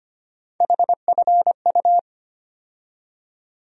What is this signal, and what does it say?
Morse code, heard as a keyed tone, about 25 wpm, 700 Hz. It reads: HFU
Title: USA Call Sign Postfixes - 3 Letters 25wpm